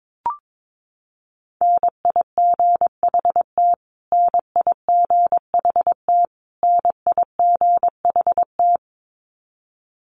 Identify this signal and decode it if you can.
Morse code, heard as a keyed tone, about 22 wpm, 700 Hz. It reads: NIGHT NIGHT NIGHT
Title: Top 500 Words Repeated 22wpm